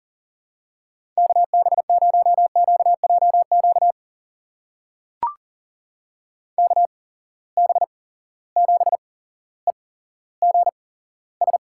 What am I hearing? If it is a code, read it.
Morse code, heard as a keyed tone, about 40 wpm, spaced as for 17 wpm, 700 Hz. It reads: KB0QJQ KB7EGS
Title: USA Call Signs - 6 Characters - 8x Character Spacing - 40wpm